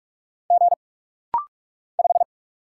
Morse code, heard as a keyed tone, about 46 wpm, 700 Hz. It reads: G 5
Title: Single Letter-Number - Rapid-Fire 45wpm